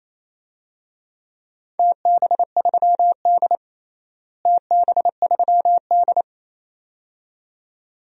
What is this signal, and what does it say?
Morse code, heard as a keyed tone, about 28 wpm, 700 Hz. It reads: TB3D TB3D
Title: DXCC-100 Call Signs - 4 Characters - Repeated 2x - 28wpm